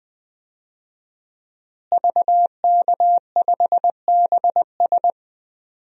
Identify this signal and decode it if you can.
Morse code, heard as a keyed tone, about 20 wpm, 700 Hz. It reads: VK5BS
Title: DXCC-100 Call Signs - 5 Characters - Repeated 2x - 20wpm